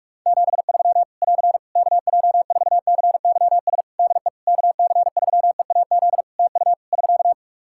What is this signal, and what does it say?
Morse code, heard as a keyed tone, about 45 wpm, 700 Hz. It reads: 73 <AR> KJ4CYS DE KC3EAZ TU <SK>
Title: Practice QSO - Exchange 4 - Ending - 45wpm